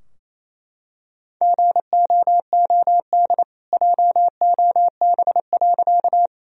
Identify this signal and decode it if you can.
Morse code, heard as a keyed tone, about 28 wpm, 700 Hz. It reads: GOOD JOB.
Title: Sentences from Top 1000 Words - Review Part 3 - 28wpm